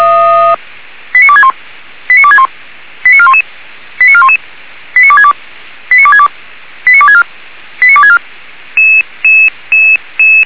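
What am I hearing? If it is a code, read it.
Five-tone selective call (ZVEI), heard as quick runs of five tones, tones of about 70 ms. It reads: 89261 89261 89411 89411 89262 89262 89264 89264
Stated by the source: ZVEI five-tone alert received by SDR, audio.f85360.feuerwehr-orlamuende.org fire brigade server.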